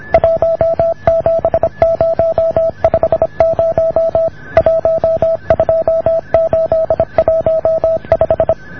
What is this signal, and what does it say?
Morse code, heard as a keyed tone, about 26 wpm, 620 Hz. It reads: 17050 12815